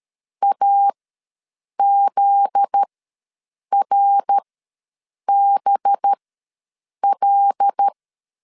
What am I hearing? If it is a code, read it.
Morse code, heard as a keyed tone, about 13 wpm, spaced as for 7 wpm, 800 Hz. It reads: AZRBL
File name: LEZIONI / 9 2 ws 12wpm 3406